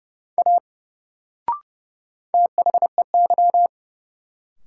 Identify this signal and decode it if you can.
Morse code, heard as a keyed tone, about 30 wpm, 700 Hz. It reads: A THEY
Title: Sentences from Top 100 Words - New Word Drill - 30wpm